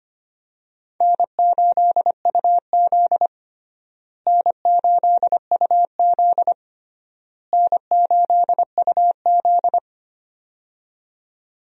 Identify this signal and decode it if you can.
Morse code, heard as a keyed tone, about 25 wpm, 700 Hz. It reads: N8UZ N8UZ N8UZ
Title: USA Call Signs - 4 Characters - Repeated 3x - 25wpm